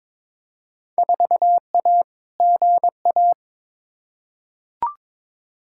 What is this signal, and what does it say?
Morse code, heard as a keyed tone, about 22 wpm, 700 Hz. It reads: 4A GA
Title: Field Day Exchanges - 22wpm